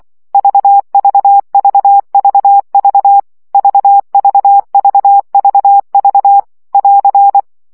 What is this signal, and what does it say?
Morse code, heard as a keyed tone, about 24 wpm, 800 Hz. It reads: VVVVV VVVVV <AR>
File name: CW gyakorlas v 20 baud